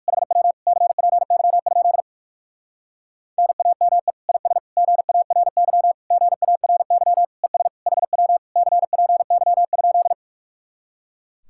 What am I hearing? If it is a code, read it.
Morse code, heard as a keyed tone, about 58 wpm, 700 Hz. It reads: HW CPY? NAME IS GARY GARY ES HW CPY?